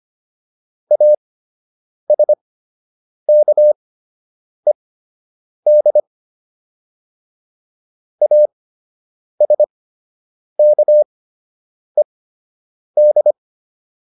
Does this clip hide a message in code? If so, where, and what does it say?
Yes: Morse code from 0.9 to 13.3 s, heard as a keyed tone, about 25 wpm, spaced as for 8 wpm, 600 Hz. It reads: ASKED ASKED